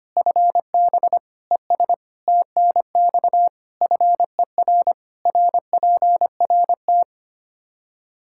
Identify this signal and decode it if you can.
Morse code, heard as a keyed tone, about 25 wpm, 700 Hz. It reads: FB ES TNX FER RPRT
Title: Practice QSO - Exchange 2 - Info - 25wpm